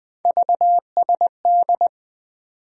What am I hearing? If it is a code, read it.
Morse code, heard as a keyed tone, about 21 wpm, 700 Hz. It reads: VSD
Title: USA Call Sign Postfixes - 3 Letters 20wpm